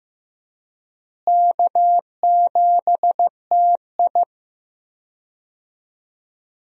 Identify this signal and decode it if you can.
Morse code, heard as a keyed tone, about 15 wpm, 700 Hz. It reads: K7TI
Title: USA Call Signs - 4 Characters - Repeated 3x - 15wpm